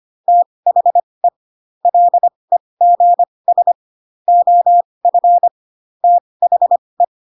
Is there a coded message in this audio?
Yes: Morse code from 0.3 to 7.0 s, heard as a keyed tone, about 25 wpm, spaced as for 20 wpm, 700 Hz. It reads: THE LEGS OF THE